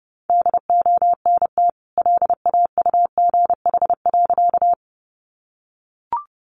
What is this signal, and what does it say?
Morse code, heard as a keyed tone, about 30 wpm, 700 Hz. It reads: DONT LAUGH.
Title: Sentences from Top 500 Words 30wpm